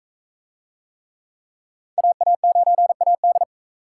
Morse code, heard as a keyed tone, about 39 wpm, 700 Hz. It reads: AA9AD